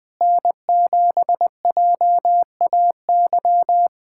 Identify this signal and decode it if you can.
Morse code, heard as a keyed tone, about 20 wpm, 700 Hz. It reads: N7JAY
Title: USA Callsign DE Callsign - 5 Characters - 20wpm